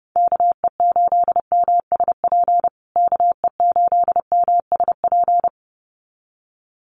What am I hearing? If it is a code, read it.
Morse code, heard as a keyed tone, about 30 wpm, 700 Hz. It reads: KE8MSP KE8MSP
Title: USA Call Signs Repeated 2x 30wpm